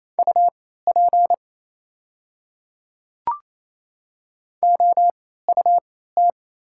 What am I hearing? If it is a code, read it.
Morse code, heard as a keyed tone, about 28 wpm, spaced as for 16 wpm, 700 Hz. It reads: UP OUT
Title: 3x Character Spacing - Top 100 Words - 28wpm